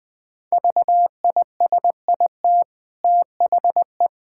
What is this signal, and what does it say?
Morse code, heard as a keyed tone, about 20 wpm, 700 Hz. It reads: VISIT THE
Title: Easy Sentences with Hints - 20wpm